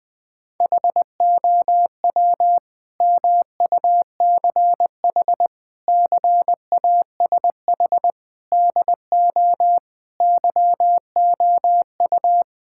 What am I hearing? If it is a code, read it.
Morse code, heard as a keyed tone, about 20 wpm, 700 Hz. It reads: HOW MUCH CASH DO YOU
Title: Sentences from Top 1000 Words - Review Part 5 - 20wpm